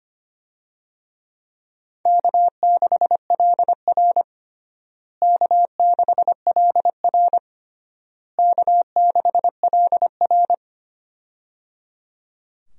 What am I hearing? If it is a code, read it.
Morse code, heard as a keyed tone, about 25 wpm, 700 Hz. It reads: K6LR K6LR K6LR
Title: USA Call Signs - 4 Characters - Repeated 3x - 25wpm